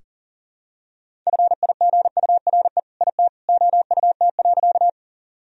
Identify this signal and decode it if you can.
Morse code, heard as a keyed tone, about 37 wpm, 700 Hz. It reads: FIGURE IT OUT.
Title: Sentences from Top 1000 Words - Review Part 2 - 40wpm